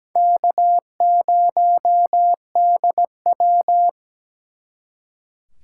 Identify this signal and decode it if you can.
Morse code, heard as a keyed tone, about 20 wpm, 700 Hz. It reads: K0DW
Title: Speed Racing - USA 1x2 Call Signs 17wpm